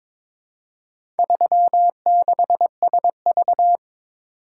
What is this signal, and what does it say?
Morse code, heard as a keyed tone, about 22 wpm, 700 Hz. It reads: 36SV
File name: practice / 22wpm / Four Letter-Number Combinations - 22wpm